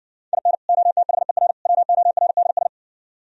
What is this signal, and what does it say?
Morse code, heard as a keyed tone, about 60 wpm, 700 Hz. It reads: IN OTHER WORDS